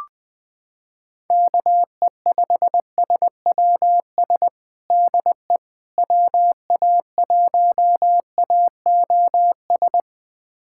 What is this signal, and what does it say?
Morse code, heard as a keyed tone, about 20 wpm, 700 Hz. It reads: KE5SWS DE WA1AOS
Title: USA Callsign DE Callsign - 6 Characters - 20wpm